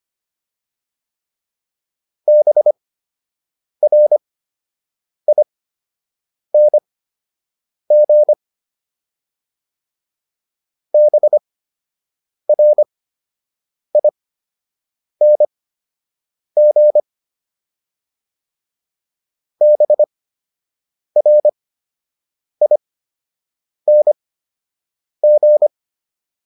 Morse code, heard as a keyed tone, about 25 wpm, spaced as for 7 wpm, 600 Hz. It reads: BRING BRING BRING